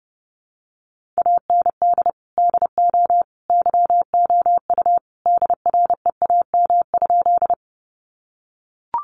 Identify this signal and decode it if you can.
Morse code, heard as a keyed tone, about 30 wpm, 700 Hz. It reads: AND DO YOU DREAM?
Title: Sentences from Top 1500 Words 30wpm